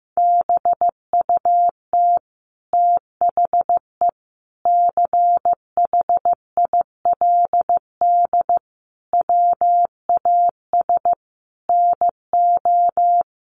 Morse code, heard as a keyed tone, about 15 wpm, 700 Hz. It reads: BUT THE CHILD WAS NO
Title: Sentences from Top 1500 Words 15wpm